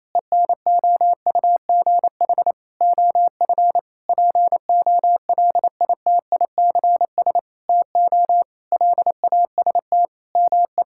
Morse code, heard as a keyed tone, about 28 wpm, 700 Hz. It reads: ENOUGH OF POLITICS TO LAST ME
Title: Sentences from Top 1000 Words - Review Part 6 - 28wpm